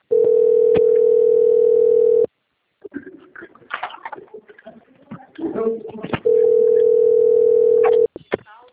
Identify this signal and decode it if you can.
Ringback tone (North America), 440 + 480 Hz, with speech.
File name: out- - - - - .2271